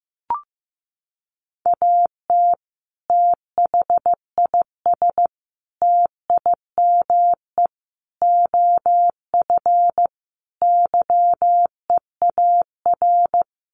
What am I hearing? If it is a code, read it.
Morse code, heard as a keyed tone, about 15 wpm, 700 Hz. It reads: AT THIS TIME OF YEAR
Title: Sets of 5 Words Encore2 15wpm